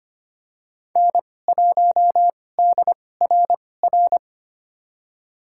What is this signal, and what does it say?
Morse code, heard as a keyed tone, about 25 wpm, spaced as for 18 wpm, 700 Hz. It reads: N1DRR
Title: USA Call Signs - 5 Characters - 2x Character Spacing - 25wpm